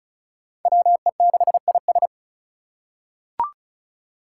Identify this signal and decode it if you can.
Morse code, heard as a keyed tone, about 36 wpm, 700 Hz. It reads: WE6IS
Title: Five Challenging Letter-Number Combinations 35wpm